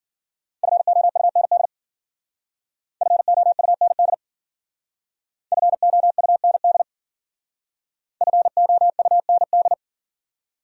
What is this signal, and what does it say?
Morse code, heard as a keyed tone, about 51 wpm, 700 Hz. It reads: FOUND FOUND FOUND FOUND